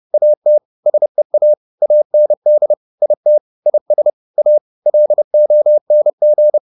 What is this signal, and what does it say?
Morse code, heard as a keyed tone, about 30 wpm, 600 Hz. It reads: AT SEA AND IT IS A LONG